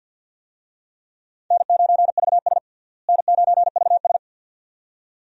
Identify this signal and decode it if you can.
Morse code, heard as a keyed tone, about 47 wpm, 700 Hz. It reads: N9VS N9VS
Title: USA Call Signs - 4 Characters - Repeated 2x - 50wpm